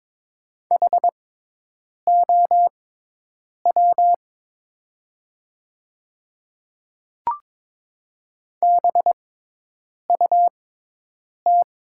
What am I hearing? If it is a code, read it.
Morse code, heard as a keyed tone, about 22 wpm, spaced as for 8 wpm, 700 Hz. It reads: HOW BUT
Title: 6x Character Spacing - Top 300 Words - 22wpm